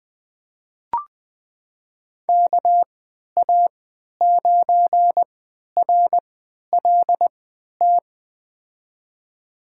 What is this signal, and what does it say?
Morse code, heard as a keyed tone, about 21 wpm, spaced as for 12 wpm, 700 Hz. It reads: KA9RLT
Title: USA Call Signs - 3x Character Spacing - 20wpm